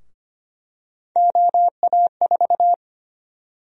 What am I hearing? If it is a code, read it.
Morse code, heard as a keyed tone, about 25 wpm, 700 Hz. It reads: OA4
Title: USA Call Sign Prefixes - 25wpm